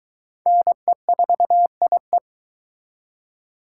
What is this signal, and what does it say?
Morse code, heard as a keyed tone, about 22 wpm, 700 Hz. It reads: NE4IE